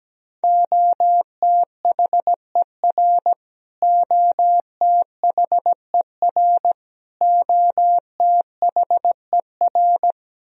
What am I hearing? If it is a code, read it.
Morse code, heard as a keyed tone, about 17 wpm, 700 Hz. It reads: OTHER OTHER OTHER
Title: Top 100 Words Repeated 17wpm